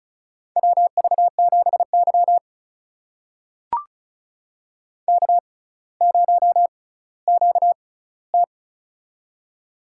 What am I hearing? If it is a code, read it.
Morse code, heard as a keyed tone, about 36 wpm, spaced as for 18 wpm, 700 Hz. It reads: WV7Y K0QT
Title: USA Call Signs - 4 Characters - 6x Character Spacing - 35wpm